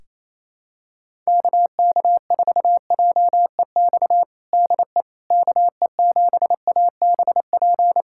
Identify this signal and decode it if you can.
Morse code, heard as a keyed tone, about 27 wpm, 700 Hz. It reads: KK4JEX DE KE7ABP
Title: USA Callsign DE Callsign - 6 Characters - 28wpm